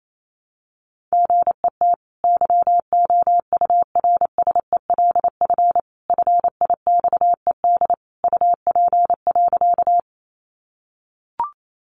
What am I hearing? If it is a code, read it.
Morse code, heard as a keyed tone, about 28 wpm, 700 Hz. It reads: GET YOURSELF FIXED UP.
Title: Sentences from Top 1100 Words 28wpm